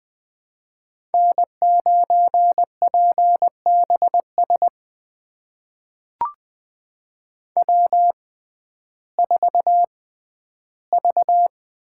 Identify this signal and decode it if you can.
Morse code, heard as a keyed tone, about 20 wpm, spaced as for 12 wpm, 700 Hz. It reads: N9PBS W4V
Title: USA Call Signs - 5 Characters - 6x Character Spacing - 20wpm